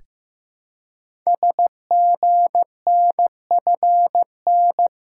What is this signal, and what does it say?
Morse code, heard as a keyed tone, about 15 wpm, 700 Hz. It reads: SGNFN
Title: Code Groups - 5 Letters - 15wpm